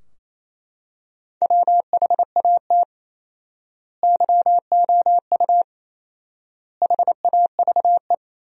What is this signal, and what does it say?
Morse code, heard as a keyed tone, about 28 wpm, 700 Hz. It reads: WHAT YOU HAVE
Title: 4x Word-Spacing - Sentences from Top 200 Words - 28wpm